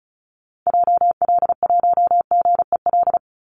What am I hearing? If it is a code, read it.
Morse code, heard as a keyed tone, about 34 wpm, 700 Hz. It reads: JL1GEL
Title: Call Signs - 35wpm